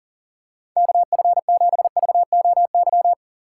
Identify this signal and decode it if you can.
Morse code, heard as a keyed tone, about 37 wpm, 700 Hz. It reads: KF7VOY
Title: USA Call Signs - 4x Character Spacing - 40wpm